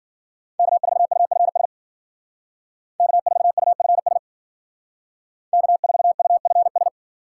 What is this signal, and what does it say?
Morse code, heard as a keyed tone, about 52 wpm, 700 Hz. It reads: K4UFS K4UFS K4UFS